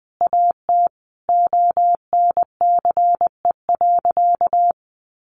Morse code, heard as a keyed tone, about 20 wpm, 700 Hz. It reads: AT ONCE.
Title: Sentences from Top 300 Words 20wpm